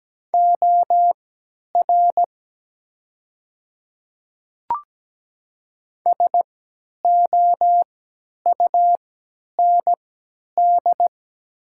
Morse code, heard as a keyed tone, about 17 wpm, spaced as for 10 wpm, 700 Hz. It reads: OR SOUND
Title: 3x Character Spacing - Top 100 Words - 17wpm